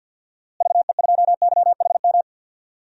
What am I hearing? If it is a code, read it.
Morse code, heard as a keyed tone, about 52 wpm, 700 Hz. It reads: VE2YHM
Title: DXCC-100 Call Signs - 6 Characters - Repeated 2x - 50wpm